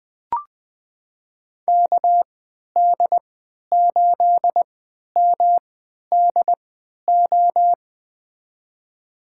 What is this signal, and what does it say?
Morse code, heard as a keyed tone, about 20 wpm, spaced as for 11 wpm, 700 Hz. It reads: KD8MDO
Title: USA Call Signs - 6 Characters - 3x Character Spacing - 20wpm